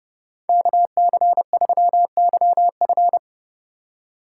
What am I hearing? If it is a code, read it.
Morse code, heard as a keyed tone, about 30 wpm, 700 Hz. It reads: KC3YF
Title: USA Call Signs - 3x Character Spacing - 30wpm